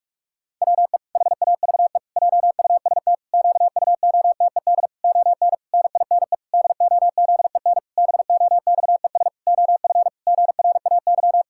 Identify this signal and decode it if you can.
Morse code, heard as a keyed tone, about 46 wpm, 700 Hz. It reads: WE HAVE JUST QUOTED ON NINE DOZEN BOXES OF GRAY
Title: Pangram Challenge v2 - 45wpm